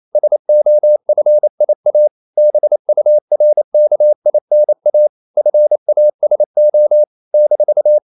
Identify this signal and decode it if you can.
Morse code, heard as a keyed tone, about 28 wpm, 600 Hz. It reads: SOFIA BURKINA FASO -
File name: CW / Practice / NationalCapitals28